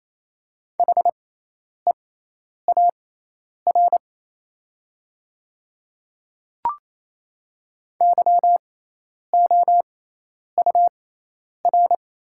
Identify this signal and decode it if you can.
Morse code, heard as a keyed tone, about 28 wpm, spaced as for 10 wpm, 700 Hz. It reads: HEAR YOUR
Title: 6x Character Spacing - Top 300 Words - 28wpm